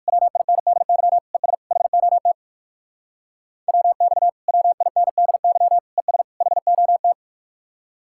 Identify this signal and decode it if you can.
Morse code, heard as a keyed tone, about 61 wpm, 700 Hz. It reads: WINDY ES HOT WX WINDY ES HOT